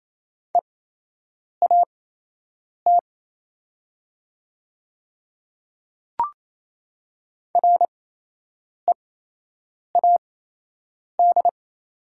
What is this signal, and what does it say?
Morse code, heard as a keyed tone, about 27 wpm, spaced as for 8 wpm, 690 Hz. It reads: EAT READ
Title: 8x Character Spacing - Top 300 Words - 28wpm